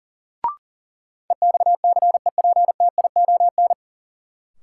Morse code, heard as a keyed tone, about 39 wpm, 700 Hz. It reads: EXCEPTION
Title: Sets of 4 Words - New Word Drill - 40wpm